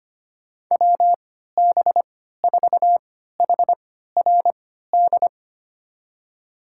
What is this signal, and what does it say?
Morse code, heard as a keyed tone, about 25 wpm, spaced as for 14 wpm, 690 Hz. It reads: WB4HRD
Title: USA Call Signs - 6 Characters - 3x Character Spacing - 25wpm